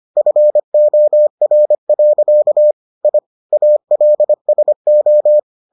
Morse code, heard as a keyed tone, about 25 wpm, 600 Hz. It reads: FOR. I ALSO